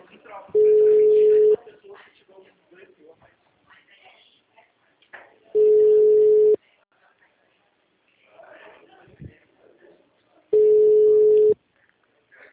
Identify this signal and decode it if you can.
Ringback tone (Europe, 425 Hz), with speech.